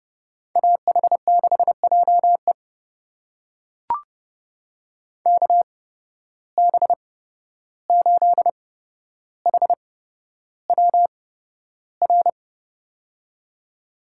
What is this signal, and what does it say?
Morse code, heard as a keyed tone, about 30 wpm, spaced as for 12 wpm, 700 Hz. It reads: AH6JE KB8HWR
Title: USA Call Signs - 8x Character Spacing - 30wpm